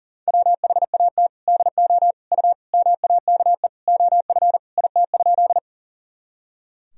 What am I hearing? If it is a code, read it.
Morse code, heard as a keyed tone, about 40 wpm, 700 Hz. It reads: WHAT DO U MAKE OF IT?
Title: Sentences from Top 100 Words and Abbreviations 40wpm